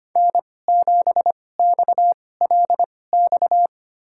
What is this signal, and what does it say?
Morse code, heard as a keyed tone, about 25 wpm, spaced as for 18 wpm, 700 Hz. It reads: N7XLX
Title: USA Call Signs - 5 Characters - 2x Character Spacing - 25wpm